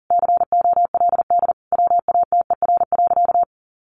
Morse code, heard as a keyed tone, about 40 wpm, 700 Hz. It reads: COLD WATER.
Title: Sentences from Top 1200 Words 40wpm